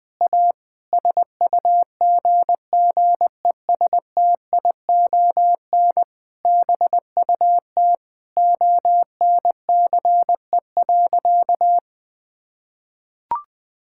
Morse code, heard as a keyed tone, about 21 wpm, 700 Hz. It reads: A SUGGESTION BUT ONCE.
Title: Sentences from Top 1600 Words 20wpm